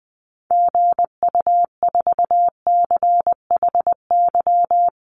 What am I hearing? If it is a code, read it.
Morse code, heard as a keyed tone, about 20 wpm, 700 Hz. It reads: GU4CHY
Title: Call Signs Repeated 3x - 20wpm